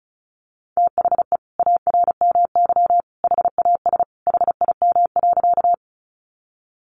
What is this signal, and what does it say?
Morse code, heard as a keyed tone, about 34 wpm, 700 Hz. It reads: THE ARMY HAS HIM.
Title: Sentences from Top 800 Words 35wpm